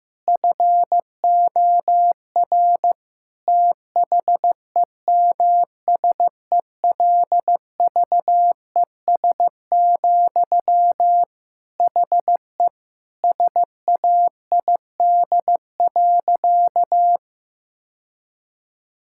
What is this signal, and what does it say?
Morse code, heard as a keyed tone, about 15 wpm, 700 Hz. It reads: FOR THEMSELVES, HE SAID.
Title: Sentences from Top 1000 Words - Review Part 6 - 15wpm